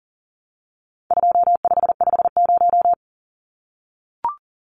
Morse code, heard as a keyed tone, about 40 wpm, 700 Hz. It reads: 2550
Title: Numbers - 4-Digit 40wpm